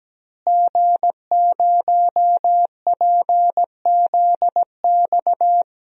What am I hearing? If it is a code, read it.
Morse code, heard as a keyed tone, about 17 wpm, 700 Hz. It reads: G0PZX
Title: DXCC-100 Call Signs - 5 Characters - Repeated 3x - 17wpm